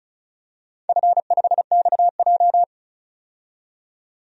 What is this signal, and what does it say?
Morse code, heard as a keyed tone, about 36 wpm, 700 Hz. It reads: F5XJ
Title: DXCC-100 Call Signs - 4 Characters - 35wpm